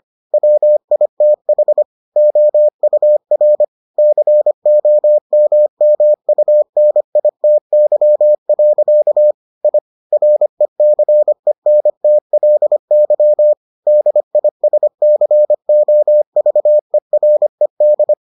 Morse code, heard as a keyed tone, about 25 wpm, 600 Hz. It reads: WITH OUR COMMUNITY. I RECENTLY DISCOVERED